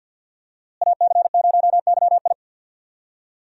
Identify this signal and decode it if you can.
Morse code, heard as a keyed tone, about 47 wpm, 700 Hz. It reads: AC0YI
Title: USA Call Signs - 5 Characters - Repeated 2x - 50wpm